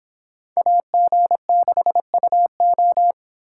Morse code, heard as a keyed tone, about 31 wpm, 700 Hz. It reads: AG6UO